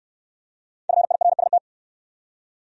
Morse code, heard as a keyed tone, about 62 wpm, 700 Hz. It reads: FIRST